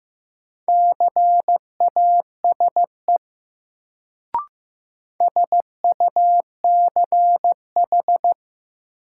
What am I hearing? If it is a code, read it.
Morse code, heard as a keyed tone, about 15 wpm, 700 Hz. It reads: CASE SUCH
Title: Sets of 2 Words - New Word Drill - 15wpm